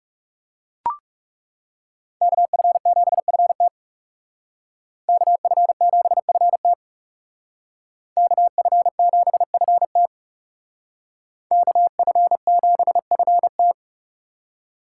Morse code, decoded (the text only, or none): KF7FT KF7FT KF7FT KF7FT